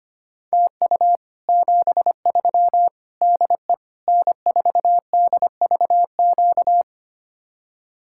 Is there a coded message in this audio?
Yes: Morse code from 0.5 to 6.8 s, heard as a keyed tone, about 33 wpm, 700 Hz. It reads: TU 73 DE N4DVQ